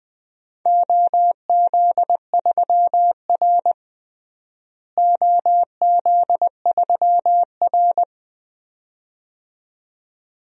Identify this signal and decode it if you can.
Morse code, heard as a keyed tone, about 20 wpm, 700 Hz. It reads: OZ3R OZ3R
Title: DXCC-100 Call Signs - 4 Characters - Repeated 2x - 20wpm